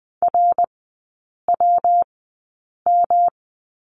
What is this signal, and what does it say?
Morse code, heard as a keyed tone, about 20 wpm, spaced as for 8 wpm, 700 Hz. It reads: RWM
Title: 114 Lesson 19e - Letter w - Three Characters 20wpm